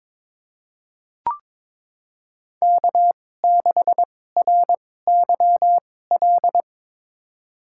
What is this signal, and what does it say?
Morse code, heard as a keyed tone, about 22 wpm, spaced as for 16 wpm, 700 Hz. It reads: K6RYL
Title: USA Call Signs - 5 Characters - 2x Character Spacing - 22wpm